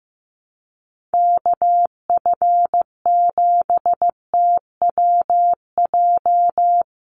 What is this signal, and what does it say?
Morse code, heard as a keyed tone, about 15 wpm, 700 Hz. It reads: KF7TWJ
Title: USA Call Signs 15wpm